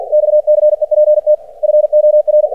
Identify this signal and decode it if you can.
Morse code, heard as a keyed tone, about 50 wpm, 600 Hz. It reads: XCEPT FOR
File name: archived / CRSnet